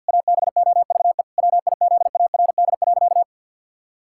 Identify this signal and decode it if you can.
Morse code, heard as a keyed tone, about 50 wpm, 700 Hz. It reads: ABOVE WIZARD.
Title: Pangram Challenge v2 - 50wpm